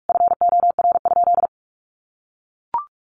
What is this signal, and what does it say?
Morse code, heard as a keyed tone, about 44 wpm, 700 Hz. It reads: FOR?
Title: Sentences from Top 1200 Words 45wpm